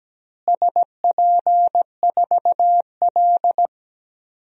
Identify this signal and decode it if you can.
Morse code, heard as a keyed tone, about 17 wpm, 700 Hz. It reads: SP4L
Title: DXCC-100 Call Signs - 4 Characters - Repeated 3x - 17wpm